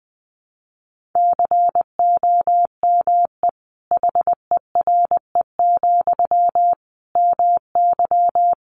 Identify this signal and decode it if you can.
Morse code, heard as a keyed tone, about 20 wpm, 700 Hz. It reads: COME HERE, MY
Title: Sentences from Top 300 Words 20wpm